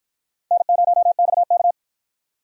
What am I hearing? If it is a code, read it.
Morse code, heard as a keyed tone, about 53 wpm, 700 Hz. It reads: N0XK